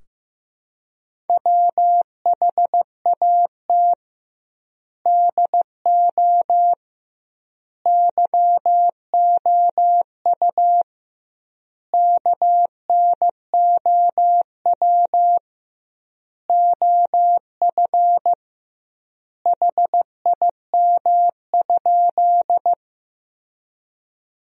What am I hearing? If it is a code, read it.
Morse code, heard as a keyed tone, about 15 wpm, 700 Hz. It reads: WHAT DO YOU KNOW OF HIM?
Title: 2x Word-Spacing - Sentences from Top 100 Words - 15wpm